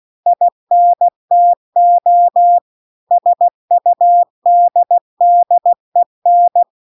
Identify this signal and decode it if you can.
Morse code, heard as a keyed tone, about 16 wpm, 700 Hz. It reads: INTO SUDDEN
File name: CW-The-Big-Twoheaded-River-2-16WPM